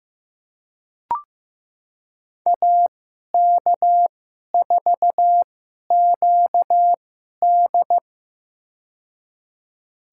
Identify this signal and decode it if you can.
Morse code, heard as a keyed tone, about 15 wpm, spaced as for 11 wpm, 700 Hz. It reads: AK4QD
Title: USA Call Signs - 5 Characters - 2x Character Spacing - 15wpm